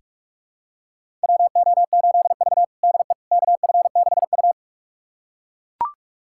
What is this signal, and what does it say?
Morse code, heard as a keyed tone, about 41 wpm, 700 Hz. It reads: WO8V DE KF6U
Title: USA Callsign DE Callsign - 4 Characters - 45wpm